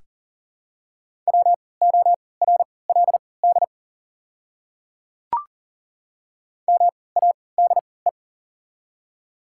Morse code, heard as a keyed tone, about 40 wpm, spaced as for 23 wpm, 690 Hz. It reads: WORLD MADE
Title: 3x Character Spacing - Top 200 Words - 40wpm